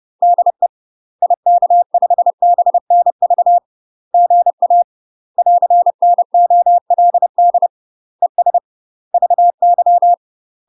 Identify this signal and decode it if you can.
Morse code, heard as a keyed tone, about 30 wpm, 700 Hz. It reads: DE IK5BNV GA <AR>NOLD ES VY
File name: UK MASTER QSOs TRAINING / 30WPM / UK MASTER QSOs 0047